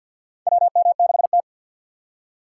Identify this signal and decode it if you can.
Morse code, heard as a keyed tone, about 47 wpm, 700 Hz. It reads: WM6T
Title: USA Call Signs - 3x Character Spacing - 50wpm